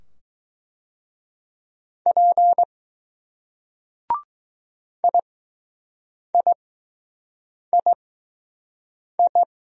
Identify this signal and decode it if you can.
Morse code, heard as a keyed tone, about 20 wpm, spaced as for 6 wpm, 700 Hz. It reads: P IIII